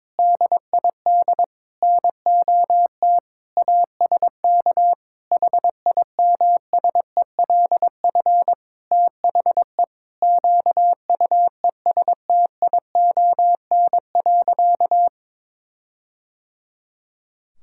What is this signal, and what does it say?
Morse code, heard as a keyed tone, about 22 wpm, 700 Hz. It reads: DID NOT ASK HIMSELF THE QUESTION.
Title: Sentences from Top 1600 Words 22wpm